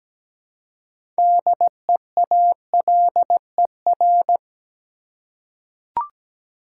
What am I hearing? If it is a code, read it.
Morse code, heard as a keyed tone, about 17 wpm, 700 Hz. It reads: DEALER
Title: Six Letter Words - 17wpm